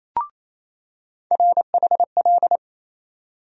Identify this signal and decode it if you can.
Morse code, heard as a keyed tone, about 28 wpm, 700 Hz. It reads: RHL
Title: USA Call Sign Postfixes - 3 Letters 28wpm